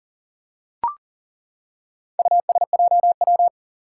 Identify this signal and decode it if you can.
Morse code, heard as a keyed tone, about 40 wpm, 700 Hz. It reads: USJW
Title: Four Challenging Letter-Number Combinations 40wpm